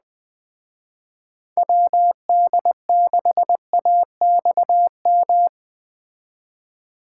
Morse code, heard as a keyed tone, about 20 wpm, 700 Hz. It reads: WD6AXM